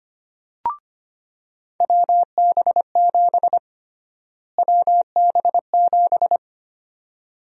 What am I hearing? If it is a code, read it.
Morse code, heard as a keyed tone, about 25 wpm, 700 Hz. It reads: WB7 WB7
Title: USA Call Sign Prefixes - Repeated 2x - 25wpm